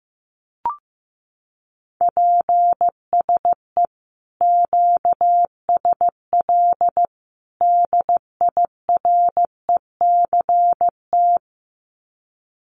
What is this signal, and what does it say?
Morse code, heard as a keyed tone, about 15 wpm, 700 Hz. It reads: PSE QSL DIRECT
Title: Common QSO Phrases 15wpm